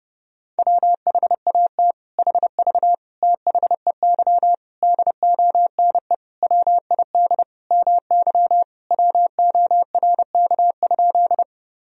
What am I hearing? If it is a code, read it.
Morse code, heard as a keyed tone, about 30 wpm, 700 Hz. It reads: WHAT HV THEY DONE WID MY WORK?
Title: Sentences from Top 500 Words and Abbreviations 30wpm